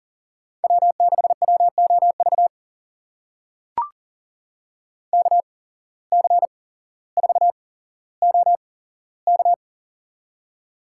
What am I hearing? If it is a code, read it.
Morse code, heard as a keyed tone, about 38 wpm, spaced as for 17 wpm, 700 Hz. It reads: W6WOV KC4OK
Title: USA Call Signs - 5 Characters - 8x Character Spacing - 40wpm